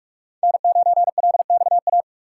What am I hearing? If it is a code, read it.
Morse code, heard as a keyed tone, about 43 wpm, 700 Hz. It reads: N9LXA
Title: USA Call Signs - 5 Characters - 3x Character Spacing - 45wpm